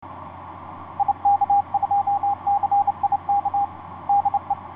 Morse code, heard as a keyed tone, about 27 wpm, 810 Hz, with radio noise. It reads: IK2CIK DE
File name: ys1zc ik2cik etc 30m cw 07feb2010 2249